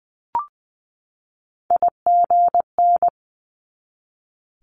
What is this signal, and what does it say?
Morse code, heard as a keyed tone, about 20 wpm, 700 Hz. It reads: IGN
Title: Multi-letter Phonemes 20wpm